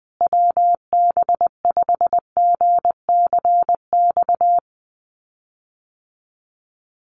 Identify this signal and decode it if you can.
Morse code, heard as a keyed tone, about 19 wpm, 700 Hz. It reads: WB5GCX
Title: USA Call Signs Repeated 2x 20wpm